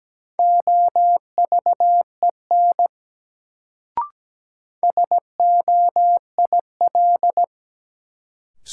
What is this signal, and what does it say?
Morse code, heard as a keyed tone, about 17 wpm, 700 Hz. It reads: OVEN SOIL
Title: Sentences from Top 1000 Words - Review Part 4 - New Word Drill - 17wpm